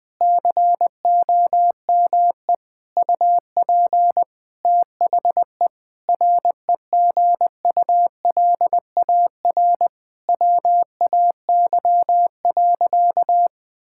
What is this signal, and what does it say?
Morse code, heard as a keyed tone, about 20 wpm, 700 Hz. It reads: COME UP THE REGULAR WAY.
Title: Sentences from Top 1000 Words - Review Part 6 - 20wpm